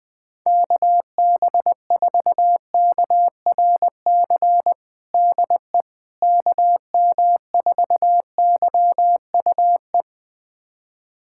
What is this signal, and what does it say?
Morse code, heard as a keyed tone, about 20 wpm, 700 Hz. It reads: KB4KRC DE KM4YUE
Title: USA Callsign DE Callsign - 6 Characters - 20wpm